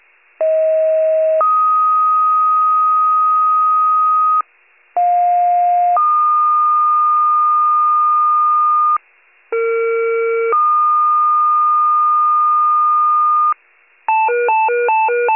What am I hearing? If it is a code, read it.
Two-tone page, heard as a one-second tone, then a three-second tone, about 640 then 1250 Hz, about 710 then 1220 Hz, about 460 then 1220 Hz. It is heard from 0.4 to 13.5 s.
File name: pager / 46 RESCUE / wcalert